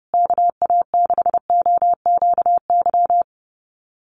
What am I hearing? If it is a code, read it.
Morse code, heard as a keyed tone, about 30 wpm, 700 Hz. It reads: KA6OQY
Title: USA Call Signs Repeated 2x 30wpm